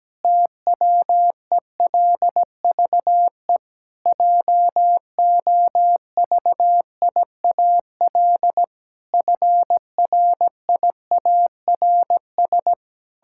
Morse code, heard as a keyed tone, about 17 wpm, 700 Hz. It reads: TWELVE JOVIAL FRIARS
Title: Pangram Challenge v2 - 17wpm